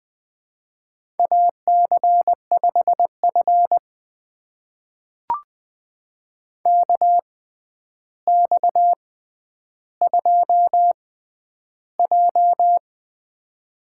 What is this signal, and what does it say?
Morse code, heard as a keyed tone, about 20 wpm, spaced as for 10 wpm, 700 Hz. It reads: AC5F KX2J
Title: USA Call Signs - 4 Characters - 6x Character Spacing - 20wpm